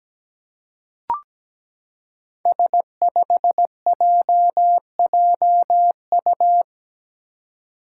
Morse code, heard as a keyed tone, about 17 wpm, 700 Hz. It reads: S5JJU
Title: Five Challenging Letter-Number Combinations 17wpm